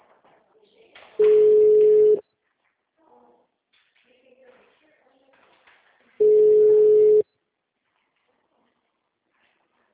Ringback tone (Europe, 425 Hz).